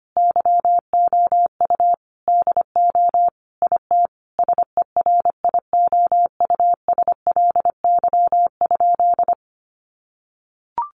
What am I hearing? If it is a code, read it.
Morse code, heard as a keyed tone, about 25 wpm, 700 Hz. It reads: YOU DO IT SERIOUSLY?
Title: Sentences from Top 1200 Words 25wpm